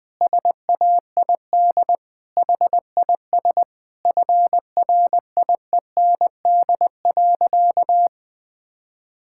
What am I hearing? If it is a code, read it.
Morse code, heard as a keyed tone, about 20 wpm, 700 Hz. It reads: SAID HIS FRIEND.
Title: Sentences from Top 1000 Words - Review Part 5 - 20wpm